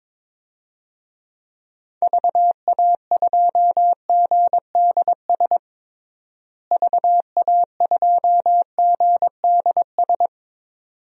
Morse code, heard as a keyed tone, about 22 wpm, 700 Hz. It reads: VA2GDS VA2GDS
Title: DXCC-100 Call Signs - 6 Characters - Repeated 2x - 22wpm